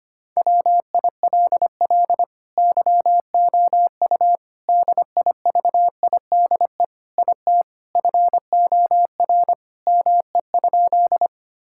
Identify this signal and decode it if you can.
Morse code, heard as a keyed tone, about 25 wpm, 700 Hz. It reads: WILL YOU DIVIDE IT FOR ME?
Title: Sentences from Top 1600 Words 25wpm